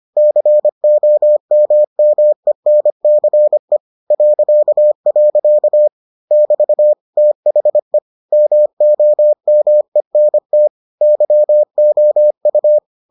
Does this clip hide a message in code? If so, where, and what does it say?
Yes: Morse code from 0.2 to 12.8 s, heard as a keyed tone, about 25 wpm, 600 Hz. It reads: COMMENCE .. <BT> THE MOMENT YOU